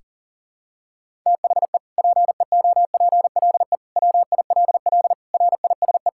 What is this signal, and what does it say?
Morse code, heard as a keyed tone, about 40 wpm, 700 Hz. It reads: THE PEOPLE WILL RISE
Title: Sentences from Top 1000 Words - Review Part 6 - 40wpm